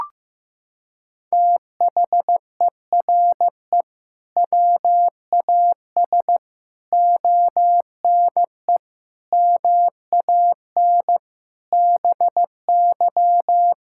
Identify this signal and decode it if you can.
Morse code, heard as a keyed tone, about 15 wpm, 700 Hz. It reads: THERE WAS ONE MAN BY
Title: Sentences from Top 1600 Words 15wpm